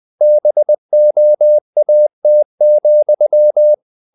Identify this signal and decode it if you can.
Morse code, heard as a keyed tone, about 20 wpm, 600 Hz. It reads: BOAT,
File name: the old man and the sea part 2 20wpm